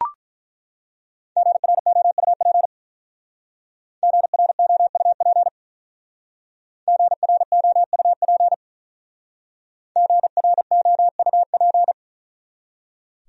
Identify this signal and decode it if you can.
Morse code, heard as a keyed tone, about 44 wpm, 700 Hz. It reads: GROUP GROUP GROUP GROUP
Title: Speed-Racing Top 300 Words 35wpm